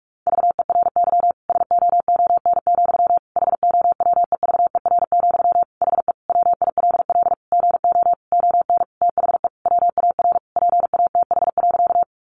Morse code, heard as a keyed tone, about 44 wpm, 700 Hz. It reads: VERY SOON, HOWEVER, HE WILL GO ON THE WAR PATH.
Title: Sentences from Top 1400 Words 45wpm